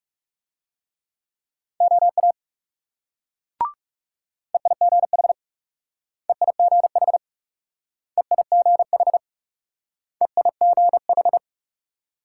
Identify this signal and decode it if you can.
Morse code, heard as a keyed tone, about 40 wpm, 700 Hz. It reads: OA EIGH EIGH EIGH EIGH